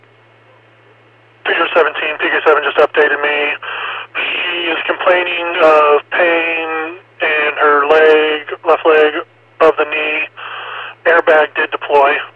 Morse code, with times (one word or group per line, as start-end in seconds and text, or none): none